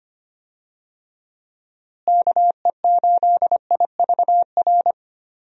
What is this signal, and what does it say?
Morse code, heard as a keyed tone, about 25 wpm, 700 Hz. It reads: KE8IVR
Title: USA Call Signs - 6 Characters - Repeated 2x - 25wpm